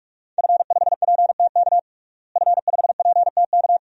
Morse code, heard as a keyed tone, about 46 wpm, 700 Hz. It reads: F5PTK F5PTK
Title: DXCC-100 Call Signs - 5 Characters - Repeated 2x - 45wpm